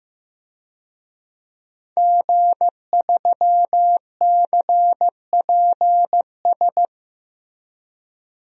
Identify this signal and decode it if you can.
Morse code, heard as a keyed tone, about 15 wpm, 700 Hz. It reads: G3CPS
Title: DXCC-100 Call Signs - 5 Characters - Repeated 2x - 15wpm